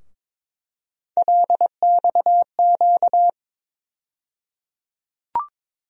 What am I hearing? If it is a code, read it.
Morse code, heard as a keyed tone, about 22 wpm, 700 Hz. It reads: LXQ
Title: USA Call Sign Postfixes - 3 Letters 22wpm